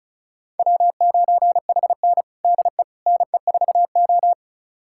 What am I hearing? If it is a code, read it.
Morse code, heard as a keyed tone, about 36 wpm, 700 Hz. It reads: W9HN DE NE4O
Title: USA Callsign DE Callsign - 4 Characters - 35wpm